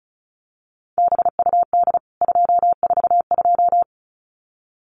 Morse code, heard as a keyed tone, about 34 wpm, 700 Hz. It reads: BUD 242
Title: CWT Exchange 35wpm